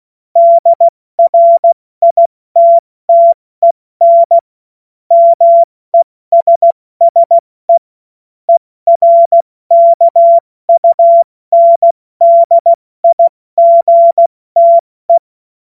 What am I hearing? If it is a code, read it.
Morse code, heard as a keyed tone, about 16 wpm, spaced as for 14 wpm, 680 Hz. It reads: DRITTEN MESSE ERKUNDIGTE
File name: Morsekurs / Witze / Witz 17.Oktober 2023 70 F80